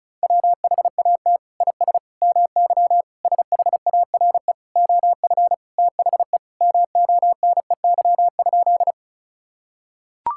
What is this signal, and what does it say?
Morse code, heard as a keyed tone, about 36 wpm, 700 Hz. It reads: WHAT IS MY SHARE OF THE MONEY?
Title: Sentences from Top 1000 Words - Review Part 6 - 35wpm